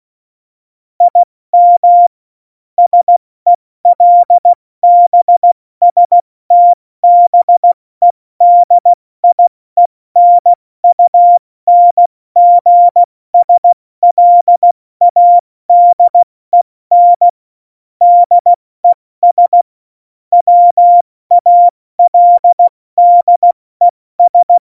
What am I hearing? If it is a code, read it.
Morse code, heard as a keyed tone, about 16 wpm, spaced as for 14 wpm, 700 Hz. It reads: IM SELBSTBEDIENUNGSLADEN DES WALDES
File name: Morsekurs / Witze / Witz 13. November 2018, 70 F80